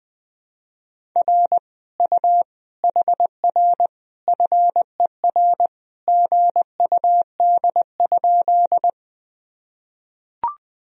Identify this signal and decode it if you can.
Morse code, heard as a keyed tone, about 20 wpm, 700 Hz. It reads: R U HR FER GUD?
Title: Sentences from Top 200 Words and Abbreviations 20wpm